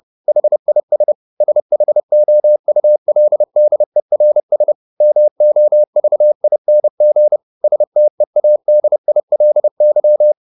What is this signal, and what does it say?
Morse code, heard as a keyed tone, about 30 wpm, 600 Hz. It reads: HIS SHOULDERS MOVING STEADILY
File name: the old man and the sea / the old man and the sea 1 30wpm